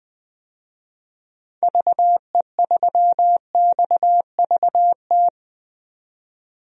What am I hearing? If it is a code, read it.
Morse code, heard as a keyed tone, about 20 wpm, 700 Hz. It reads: VE3XVT
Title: DXCC-100 Call Signs - 6 Characters - 20wpm